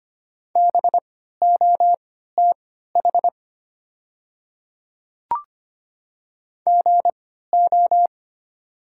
Morse code, heard as a keyed tone, about 25 wpm, spaced as for 14 wpm, 700 Hz. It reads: BOTH GO
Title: 3x Character Spacing - Top 300 Words - 25wpm